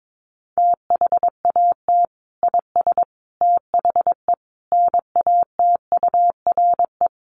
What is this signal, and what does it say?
Morse code, heard as a keyed tone, about 22 wpm, 700 Hz. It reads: THAT IS THE NATURE
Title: Sentences from Top 1100 Words 22wpm